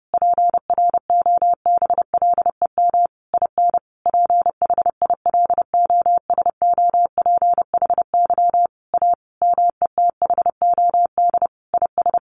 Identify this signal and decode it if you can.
Morse code, heard as a keyed tone, about 30 wpm, 700 Hz. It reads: PROBLEM IN PHILOSOPHY A METHOD IS
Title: Sentences from Top 1400 Words 30wpm